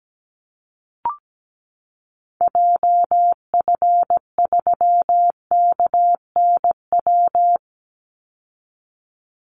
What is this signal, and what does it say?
Morse code, heard as a keyed tone, about 17 wpm, 700 Hz. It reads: JF3KNW
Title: Call Signs Repeated 3x - 17wpm